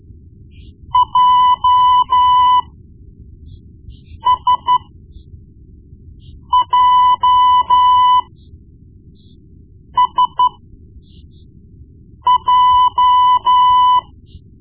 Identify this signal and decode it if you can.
Morse code, heard as a keyed tone, about 12 wpm, spaced as for 4 wpm, 1000 Hz. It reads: JSJSJ
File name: Morse Lessons / Morse Lessons / Tape-1B